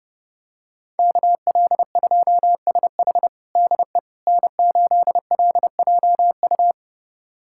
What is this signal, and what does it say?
Morse code, heard as a keyed tone, about 30 wpm, 700 Hz. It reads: KL2SH DE N8LJU
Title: USA Callsign DE Callsign 30wpm